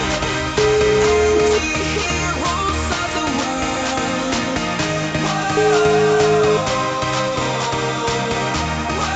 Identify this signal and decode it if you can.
Ringback tone (Europe, 425 Hz), with noise.